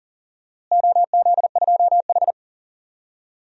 Morse code, heard as a keyed tone, about 40 wpm, 700 Hz. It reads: OZ2H
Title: DXCC-100 Call Signs - 4 Characters - Repeated 3x - 40wpm